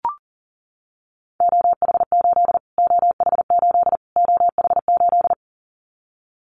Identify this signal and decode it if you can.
Morse code, heard as a keyed tone, about 40 wpm, 700 Hz. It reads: OH8 OH8 OH8
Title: Callsign Prefixes for Top 25 DXCC - Repeated 3x - 40wpm